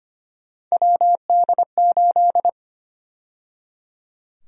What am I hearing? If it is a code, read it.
Morse code, heard as a keyed tone, about 25 wpm, 700 Hz. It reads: WD8
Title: USA Call Sign Prefixes - 25wpm